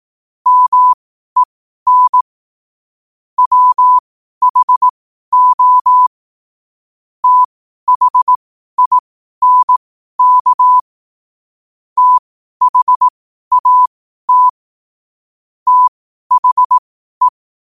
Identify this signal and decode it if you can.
Morse code, heard as a keyed tone, about 18 wpm, spaced as for 12 wpm, 1000 Hz. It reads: MEN WHO THINK THAT THE